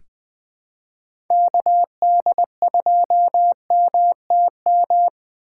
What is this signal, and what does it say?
Morse code, heard as a keyed tone, about 21 wpm, 700 Hz. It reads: KD2MTM
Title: USA Call Signs - 2x Character Spacing - 20wpm